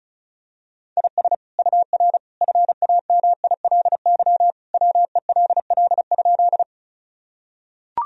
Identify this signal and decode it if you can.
Morse code, heard as a keyed tone, about 36 wpm, 700 Hz. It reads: IS UR FAMILY WELL?
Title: Sentences from Top 400 Words and Abbreviations 35wpm